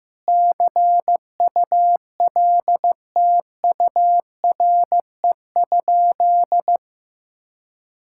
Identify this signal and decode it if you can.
Morse code, heard as a keyed tone, about 15 wpm, 700 Hz. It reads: CULTURE?
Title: Sentences from Top 1000 Words - Review Part 6 - 15wpm